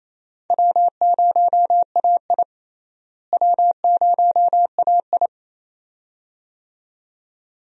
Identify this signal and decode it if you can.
Morse code, heard as a keyed tone, about 28 wpm, 700 Hz. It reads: W0AI W0AI
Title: USA Call Signs - 4 Characters - Repeated 2x - 28wpm